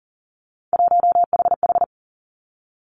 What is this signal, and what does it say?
Morse code, heard as a keyed tone, about 40 wpm, 700 Hz. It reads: 1HH
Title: Three Challenging Letter-Number Combinations 40wpm